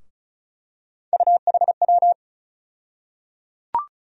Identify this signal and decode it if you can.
Morse code, heard as a keyed tone, about 34 wpm, 700 Hz. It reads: UHW
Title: USA Call Sign Postfixes - 3 Letters 35wpm